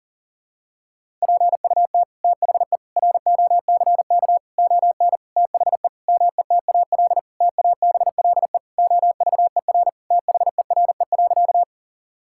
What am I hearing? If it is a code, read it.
Morse code, heard as a keyed tone, about 38 wpm, 700 Hz. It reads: PUT THE ROCK ON THE METAL TABLE OVER THERE.
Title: Sentences from Top 1000 Words - Review Part 6 - 40wpm